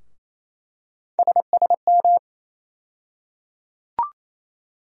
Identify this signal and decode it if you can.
Morse code, heard as a keyed tone, about 28 wpm, 700 Hz. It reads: SSM
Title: USA Call Sign Postfixes - 3 Letters 28wpm